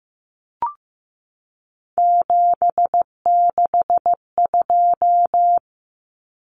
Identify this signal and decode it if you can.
Morse code, heard as a keyed tone, about 15 wpm, 700 Hz. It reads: 762
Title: Common Cut Numbers - 3-Digit 15wpm